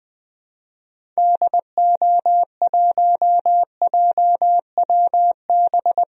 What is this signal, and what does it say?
Morse code, heard as a keyed tone, about 21 wpm, 700 Hz. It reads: DO1JWB
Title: DXCC-100 Call Signs - 6 Characters - 20wpm